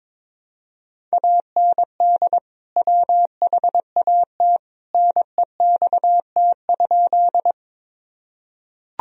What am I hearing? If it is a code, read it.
Morse code, heard as a keyed tone, about 22 wpm, 700 Hz. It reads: AND WHAT NEXT?
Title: Sentences from Top 1000 Words - Review Part 1 - 22wpm